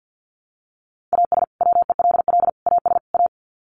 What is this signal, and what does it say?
Morse code, heard as a keyed tone, about 50 wpm, 700 Hz. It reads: AS WELL AS A
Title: Sets of 4 Words 50wpm